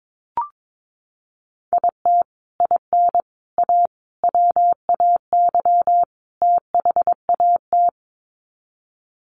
Morse code, heard as a keyed tone, about 22 wpm, 700 Hz. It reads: IT IN A WAY THAT
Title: Sets of 5 Words Encore2 22wpm